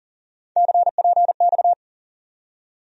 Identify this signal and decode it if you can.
Morse code, heard as a keyed tone, about 40 wpm, 700 Hz. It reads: CPX
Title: USA Call Sign Postfixes - 3 Letters 40wpm